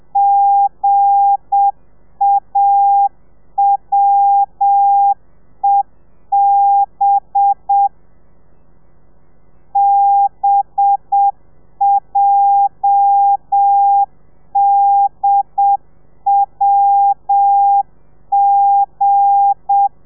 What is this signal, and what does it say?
Morse code, heard as a keyed tone, about 7 wpm, 800 Hz. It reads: GAWEB BJDWG
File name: telegrafia / i2tfj / Lez8 35